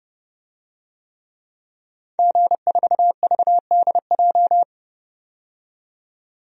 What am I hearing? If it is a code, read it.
Morse code, heard as a keyed tone, about 30 wpm, 700 Hz. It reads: G4VDJ
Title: DXCC-100 Call Signs - 5 Characters - 30wpm